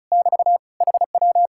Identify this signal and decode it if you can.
Morse code, heard as a keyed tone, about 46 wpm, 700 Hz. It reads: <BT> HW